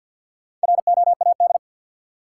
Morse code, heard as a keyed tone, about 53 wpm, 700 Hz. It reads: ROAD
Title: Sentences from Top 400 Words - New Word Drill - 50wpm